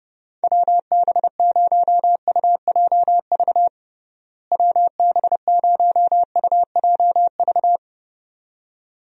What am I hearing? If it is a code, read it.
Morse code, heard as a keyed tone, about 30 wpm, 700 Hz. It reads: WB0UJV WB0UJV
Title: USA Call Signs - 6 Characters - Repeated 2x - 30wpm